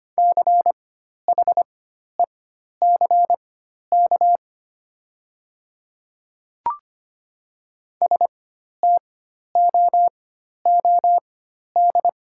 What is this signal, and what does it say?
Morse code, heard as a keyed tone, about 25 wpm, spaced as for 12 wpm, 700 Hz. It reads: CHECK STOOD
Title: 4x Character Spacing - Top 500 Words - 25wpm